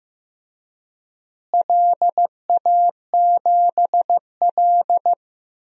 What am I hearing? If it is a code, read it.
Morse code, heard as a keyed tone, about 15 wpm, 700 Hz. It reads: LA7L
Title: DXCC-100 Call Signs - 4 Characters - Repeated 2x - 15wpm